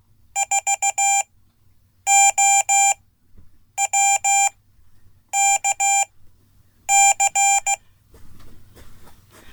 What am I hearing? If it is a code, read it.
Morse code, heard as a keyed tone, about 16 wpm, spaced as for 8 wpm, 790 Hz. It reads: 4OWKC
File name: Code / 5min-a